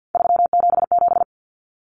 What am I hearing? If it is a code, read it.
Morse code, heard as a keyed tone, about 50 wpm, 700 Hz. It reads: 377
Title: Common Cut Numbers - 3-Digit 50wpm